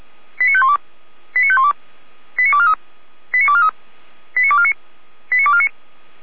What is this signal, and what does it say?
Five-tone selective call (ZVEI), heard as quick runs of five tones, tones of about 70 ms. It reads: 89512 89512 89243 89243 89249 89249
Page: ZVEI